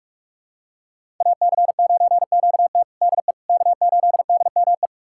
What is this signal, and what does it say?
Morse code, heard as a keyed tone, about 42 wpm, 700 Hz. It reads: AC9QT DE K8DME
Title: USA Callsign DE Callsign - 5 Characters - 45wpm